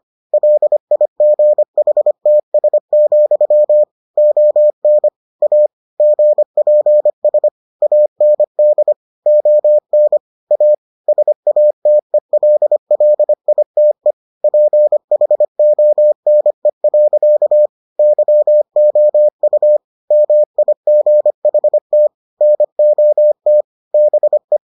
Morse code, heard as a keyed tone, about 25 wpm, 600 Hz. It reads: LIGHTS, ON A GPS AND ON A SATELLITE PHONE. YOU MIGHT NOT BE